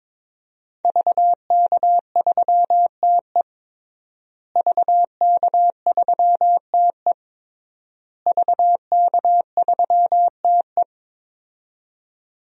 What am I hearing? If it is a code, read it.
Morse code, heard as a keyed tone, about 22 wpm, 700 Hz. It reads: VK3TE VK3TE VK3TE
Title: DXCC-100 Call Signs - 5 Characters - Repeated 3x - 22wpm